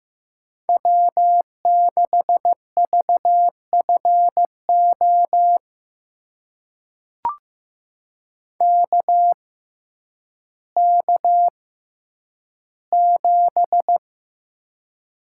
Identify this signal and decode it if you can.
Morse code, heard as a keyed tone, about 15 wpm, spaced as for 9 wpm, 700 Hz. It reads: W6VFO KK7
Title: USA Call Signs - 5 Characters - 6x Character Spacing - 15wpm